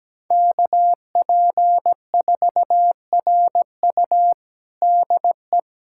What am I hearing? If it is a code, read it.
Morse code, heard as a keyed tone, about 17 wpm, 700 Hz. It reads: KP4RU DE
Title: USA Callsign DE Callsign - 5 Characters - 17wpm